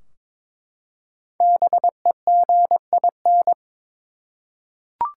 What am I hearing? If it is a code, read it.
Morse code, heard as a keyed tone, about 22 wpm, 700 Hz. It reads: BEGIN
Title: Five Letter Words - 22wpm